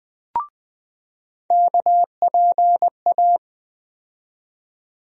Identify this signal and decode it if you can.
Morse code, heard as a keyed tone, about 20 wpm, 700 Hz. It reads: KPA
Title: USA Call Sign Postfixes - 3 Letters 20wpm